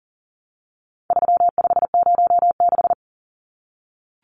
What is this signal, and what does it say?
Morse code, heard as a keyed tone, about 40 wpm, 700 Hz. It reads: 3506
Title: Numbers - 4-Digit 40wpm